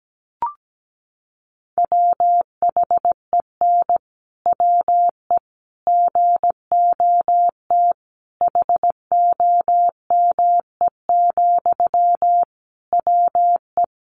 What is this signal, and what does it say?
Morse code, heard as a keyed tone, about 17 wpm, 700 Hz. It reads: WHEN WE GOT HOME, WE
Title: Sentences from Top 1400 Words 17wpm